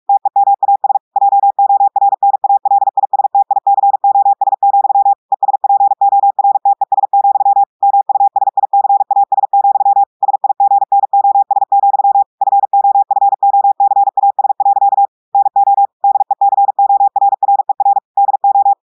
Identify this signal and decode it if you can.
Morse code, heard as a keyed tone, about 46 wpm, 800 Hz. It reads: TEMAS JORNALISTICOS, ESPORTES, MUSICAS, SIGNOS, FOFOCAS. NO DECORRER DO
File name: audios / portugais / portugais texte en clair 2 vitesse 45 mots minutes